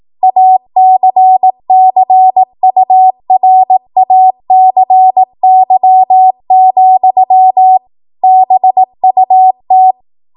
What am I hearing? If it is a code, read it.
Morse code, heard as a keyed tone, about 18 wpm, 750 Hz. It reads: ACCURACY, BUT
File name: Morse / 18WPM